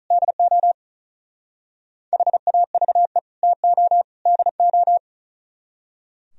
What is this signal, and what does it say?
Morse code, heard as a keyed tone, about 44 wpm, 700 Hz. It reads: DO HAVE TO DO